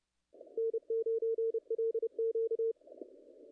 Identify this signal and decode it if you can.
Morse code, heard as a keyed tone, about 28 wpm, 440 Hz. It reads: N9LQ
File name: contests / ss-cw-k5tr-op-n5rz -0100